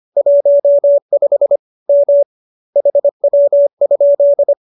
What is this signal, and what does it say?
Morse code, heard as a keyed tone, about 25 wpm, 570 Hz. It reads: 15 M HW?